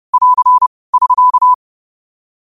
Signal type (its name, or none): Morse code